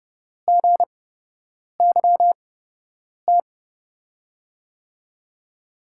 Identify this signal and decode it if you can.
Morse code, heard as a keyed tone, about 30 wpm, spaced as for 8 wpm, 700 Hz. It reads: GYT
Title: USA Call Signs - 6 Characters - 8x Character Spacing - 30wpm